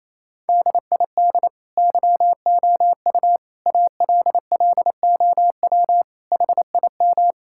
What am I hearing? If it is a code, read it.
Morse code, heard as a keyed tone, about 28 wpm, 700 Hz. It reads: DID YOU ALLOW HIM
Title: Sentences from Top 1000 Words - Review Part 5 - 28wpm